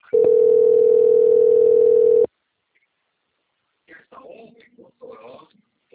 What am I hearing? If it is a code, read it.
Ringback tone (North America), 440 + 480 Hz, with speech.
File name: out- - - - - .42164